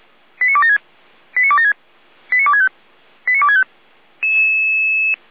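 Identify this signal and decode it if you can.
Five-tone selective call (ZVEI), heard as quick runs of five tones, tones of about 70 ms. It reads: 89276 89276 89265 89265
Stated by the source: ZVEI five-tone selective call received via SDR.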